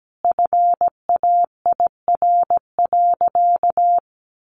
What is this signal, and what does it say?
Morse code, heard as a keyed tone, about 17 wpm, 700 Hz. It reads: FAIR.
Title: Sentences from Top 1300 Words 17wpm